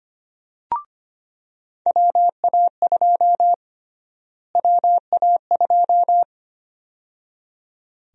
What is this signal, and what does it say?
Morse code, heard as a keyed tone, about 25 wpm, 700 Hz. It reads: WA2 WA2
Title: USA Call Sign Prefixes - Repeated 2x - 25wpm